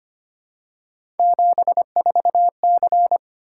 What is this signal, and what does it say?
Morse code, heard as a keyed tone, about 25 wpm, 700 Hz. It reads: 74C
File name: Three Letter-Number Combinations - 25wpm